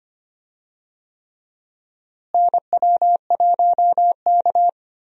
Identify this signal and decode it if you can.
Morse code, heard as a keyed tone, about 25 wpm, 700 Hz. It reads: NW1K
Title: USA Call Signs - 4 Characters - Repeated 3x - 25wpm